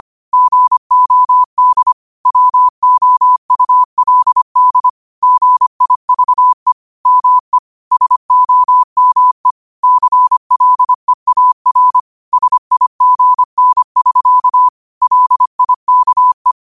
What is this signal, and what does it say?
Morse code, heard as a keyed tone, about 25 wpm, 1000 Hz. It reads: GOD WOULD GIVE ME SOME CLEAR SIGN<SK> LIKE
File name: qotdfiles / 25wpm